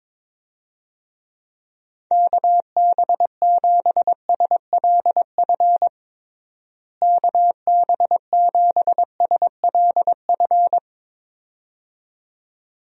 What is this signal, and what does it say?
Morse code, heard as a keyed tone, about 22 wpm, 700 Hz. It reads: KB7SLF KB7SLF
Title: USA Call Signs - 6 Characters - Repeated 2x - 22wpm